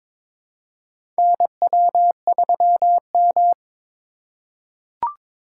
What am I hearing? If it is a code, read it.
Morse code, heard as a keyed tone, about 22 wpm, 700 Hz. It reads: NW3M
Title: USA Call Signs - 4x Character Spacing - 22wpm